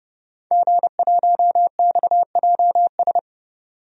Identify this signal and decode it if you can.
Morse code, heard as a keyed tone, about 30 wpm, 700 Hz. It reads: G1XJS
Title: DXCC-100 Call Signs - 5 Characters - 30wpm